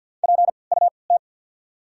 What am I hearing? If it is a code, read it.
Morse code, heard as a keyed tone, about 54 wpm, 690 Hz. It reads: P U T
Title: 3x Character Spacing - Top 200 Words - 50wpm